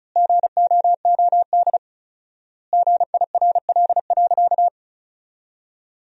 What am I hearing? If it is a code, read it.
Morse code, heard as a keyed tone, about 36 wpm, 700 Hz. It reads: GOOD GIRL.
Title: 4x Word-Spacing - Sentences from Top 400 Words - 35wpm